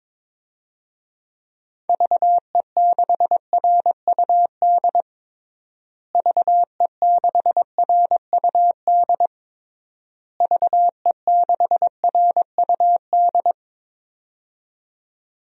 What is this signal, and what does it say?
Morse code, heard as a keyed tone, about 22 wpm, 700 Hz. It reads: VE6RUD VE6RUD VE6RUD
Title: DXCC-100 Call Signs - 6 Characters - Repeated 3x - 22wpm